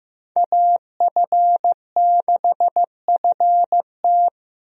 Morse code, heard as a keyed tone, about 15 wpm, 700 Hz. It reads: AF6FT
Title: USA Call Signs - 5 Characters - Repeated 3x - 15wpm